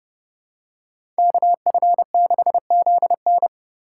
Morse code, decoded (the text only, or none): KF6ZN